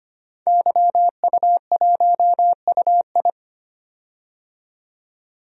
Morse code, heard as a keyed tone, about 25 wpm, 690 Hz. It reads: YU1UI
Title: DXCC-100 Call Signs - 5 Characters - 25wpm